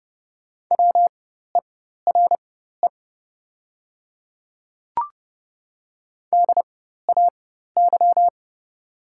Morse code, heard as a keyed tone, about 30 wpm, spaced as for 14 wpm, 690 Hz. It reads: WERE DAY
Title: 4x Character Spacing - Top 100 Words - 30wpm